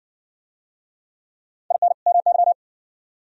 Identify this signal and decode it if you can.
Morse code, heard as a keyed tone, about 57 wpm, 700 Hz. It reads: IN MY